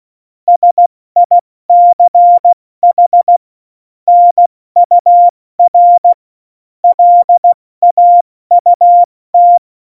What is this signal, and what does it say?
Morse code, heard as a keyed tone, about 16 wpm, 700 Hz. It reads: SICH NUR LAUT
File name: Morsekurs / Witze / Witz 17. November 2020, 70 F80